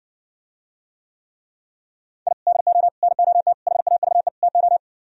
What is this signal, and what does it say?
Morse code, heard as a keyed tone, about 68 wpm, 700 Hz. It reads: I DO NOT HAVE TO